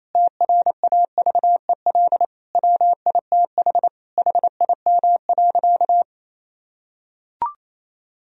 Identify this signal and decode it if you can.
Morse code, heard as a keyed tone, about 28 wpm, 700 Hz. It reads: TRAVEL WITH HIM.
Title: Sentences from Top 1000 Words - Review Part 2 - 28wpm